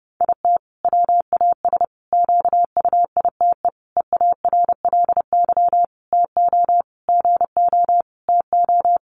Morse code, heard as a keyed tone, about 30 wpm, 700 Hz. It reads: IT WAS QUITE EARLY TO GO TO
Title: Sentences from Top 700 Words 30wpm